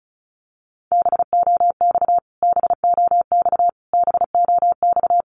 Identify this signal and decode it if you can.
Morse code, heard as a keyed tone, about 34 wpm, 700 Hz. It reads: BOX BOX BOX
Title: Top 500 Words Repeated 35wpm